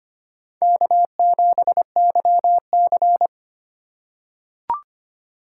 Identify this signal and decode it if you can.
Morse code, heard as a keyed tone, about 25 wpm, 700 Hz. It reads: K7YC
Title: USA Call Signs - 4 Characters - 2x Character Spacing - 25wpm